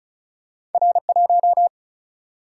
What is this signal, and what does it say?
Morse code, heard as a keyed tone, about 36 wpm, 700 Hz. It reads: R1
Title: Single-Letter Single-Number 35wpm